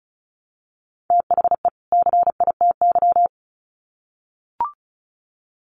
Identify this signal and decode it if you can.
Morse code, heard as a keyed tone, about 34 wpm, 700 Hz. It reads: THE CITY
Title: Sets of 2 Words 35wpm